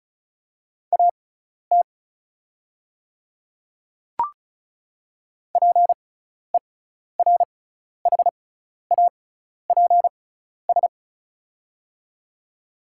Morse code, heard as a keyed tone, about 34 wpm, spaced as for 12 wpm, 690 Hz. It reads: AT PERHAPS
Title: 6x Character Spacing - Top 500 Words - 35wpm